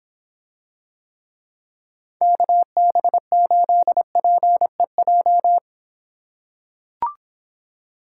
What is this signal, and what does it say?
Morse code, heard as a keyed tone, about 25 wpm, 700 Hz. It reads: KB8PEJ